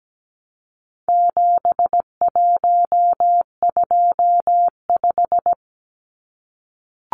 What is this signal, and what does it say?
Morse code, heard as a keyed tone, about 17 wpm, 700 Hz. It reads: 7125
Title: Numbers - 4-Digit 17wpm